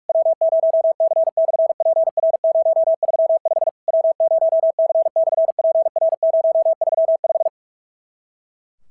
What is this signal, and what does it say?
Morse code, heard as a keyed tone, about 48 wpm, 650 Hz. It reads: W0C/PR035 W0C/PR035
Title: SOTA QSO Practice 45wpm